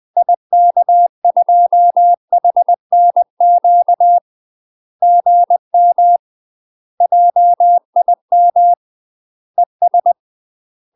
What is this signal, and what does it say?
Morse code, heard as a keyed tone, about 20 wpm, 700 Hz. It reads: IK2HNQ GM JIM ES
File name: UK MASTER QSOs TRAINING / 20WPM / UK MASTER QSOs 0023